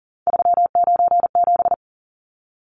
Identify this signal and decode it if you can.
Morse code, heard as a keyed tone, about 40 wpm, 700 Hz. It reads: 397
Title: Numbers - 3-Digit 40wpm